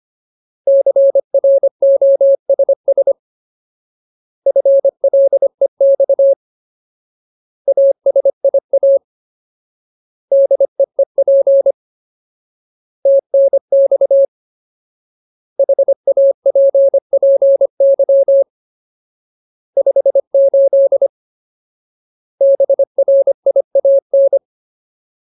Morse code, heard as a keyed tone, about 25 wpm, 550 Hz. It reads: CROSS FLEX ASIA DEEP TNX HAPPY 58 BRIAN
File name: Ultimate Word List / 25WPM Ultimate